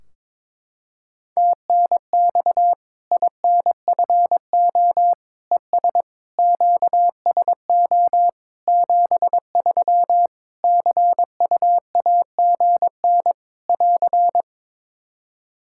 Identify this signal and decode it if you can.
Morse code, heard as a keyed tone, about 22 wpm, 700 Hz. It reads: TNX INFO ES QSO 73 CUAGN <AR>
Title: Practice QSO - Exchange 4 - Ending - 22wpm